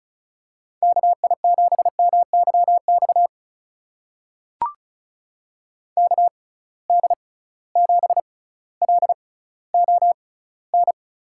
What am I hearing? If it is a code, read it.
Morse code, heard as a keyed tone, about 36 wpm, spaced as for 18 wpm, 700 Hz. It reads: KI7MYX KD7LON
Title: USA Call Signs - 6 Characters - 6x Character Spacing - 35wpm